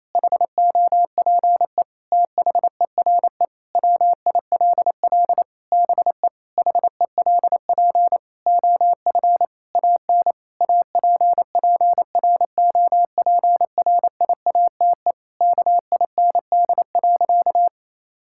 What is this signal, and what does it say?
Morse code, heard as a keyed tone, about 28 wpm, 700 Hz. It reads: HOPE THERE WILL BE HELP OF AN APPROPRIATE KIND.
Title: Sentences from Top 1000 Words - Review Part 5 - 28wpm